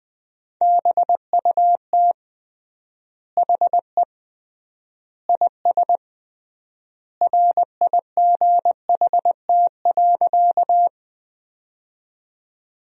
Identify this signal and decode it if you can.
Morse code, heard as a keyed tone, about 20 wpm, 700 Hz. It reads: BUT HE IS RIGHT.
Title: 3x Word-Spacing - Sentences from Top 200 Words - 20wpm